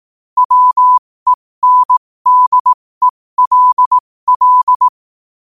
Morse code, heard as a keyed tone, about 18 wpm, spaced as for 15 wpm, 1000 Hz. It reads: WENDELL
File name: qotdfiles / 15wpm